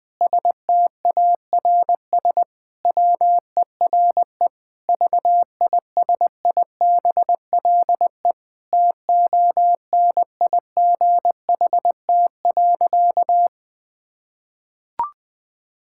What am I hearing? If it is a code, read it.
Morse code, heard as a keyed tone, about 20 wpm, 700 Hz. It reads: STARS WERE VISIBLE TONIGHT.
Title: Sentences with 2-Word Phrases - Spoken Beginning - 20wpm